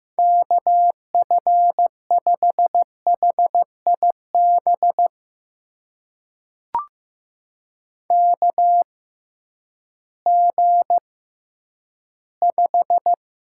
Morse code, heard as a keyed tone, about 15 wpm, spaced as for 10 wpm, 700 Hz. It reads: KF5HIB KG5
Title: USA Call Signs - 6 Characters - 6x Character Spacing - 15wpm